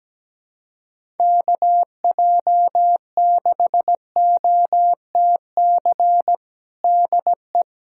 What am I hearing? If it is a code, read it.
Morse code, heard as a keyed tone, about 17 wpm, 700 Hz. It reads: KJ6OTC DE
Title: USA Callsign DE Callsign 17wpm